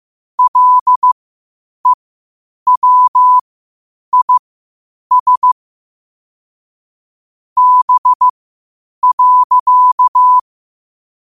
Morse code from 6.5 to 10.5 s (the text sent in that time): B.